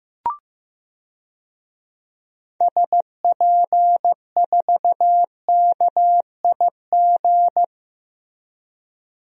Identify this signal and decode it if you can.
Morse code, heard as a keyed tone, about 16 wpm, 700 Hz. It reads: SP4KIG
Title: DXCC-100 Call Signs - 6 Characters - 15wpm